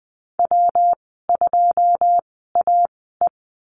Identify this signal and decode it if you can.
Morse code, heard as a keyed tone, about 20 wpm, spaced as for 15 wpm, 700 Hz. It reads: W2AE
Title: 154 Lesson 25g - Letter p - Call Signs 20wpm